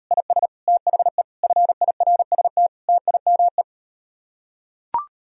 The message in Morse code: IS THE FIRST TIME

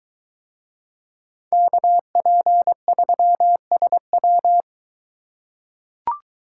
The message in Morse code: KP3SW

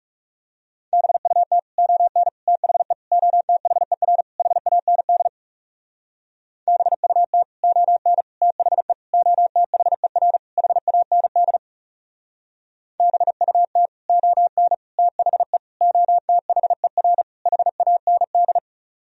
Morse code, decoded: BUT ON THE OTHER HAND BUT ON THE OTHER HAND BUT ON THE OTHER HAND

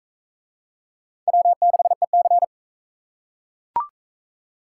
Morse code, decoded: W6EC

